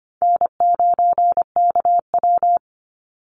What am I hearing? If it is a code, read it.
Morse code, heard as a keyed tone, about 25 wpm, 700 Hz. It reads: N9KW
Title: NA SOTA Chaser Callsigns 25wpm